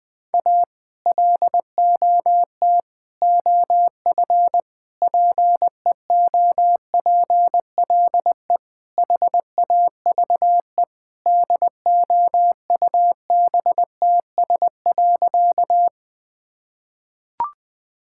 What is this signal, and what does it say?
Morse code, heard as a keyed tone, about 20 wpm, 700 Hz. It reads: A LOT OF PEOPLE HAVE DOUBTS.
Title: Sentences with 5-Word Phrases - 20wpm